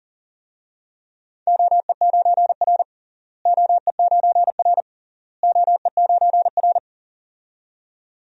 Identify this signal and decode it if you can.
Morse code, heard as a keyed tone, about 40 wpm, 700 Hz. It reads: OE9R OE9R OE9R
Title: DXCC-100 Call Signs - 4 Characters - Repeated 3x - 40wpm